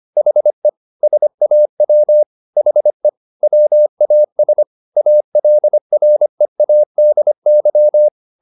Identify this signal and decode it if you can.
Morse code, heard as a keyed tone, about 25 wpm, 600 Hz. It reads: HE SAW HE WAS ALREADY
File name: the old man and the sea part 1 25wpm